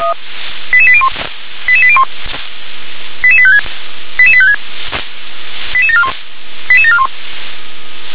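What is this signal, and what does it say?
Five-tone selective call (ZVEI), heard as quick runs of five tones, tones of about 70 ms. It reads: 88801 88801 88856 88856 88741 88741
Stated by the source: ZVEI five-tone selective call 88832 received with an SDR.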